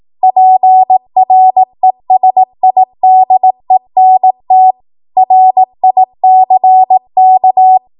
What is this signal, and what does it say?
Morse code, heard as a keyed tone, about 18 wpm, 750 Hz. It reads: PRESIDENT RICK
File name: Archive / 18 WPM / 18WPM